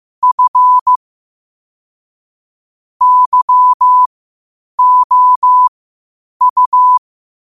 Morse code, heard as a keyed tone, about 15 wpm, spaced as for 8 wpm, 1000 Hz. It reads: F YOU
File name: qotdfiles / 8wpm